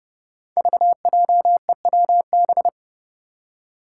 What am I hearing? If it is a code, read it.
Morse code, heard as a keyed tone, about 30 wpm, 700 Hz. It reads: VJEWB
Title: Five Challenging Letter-Number Combinations 30wpm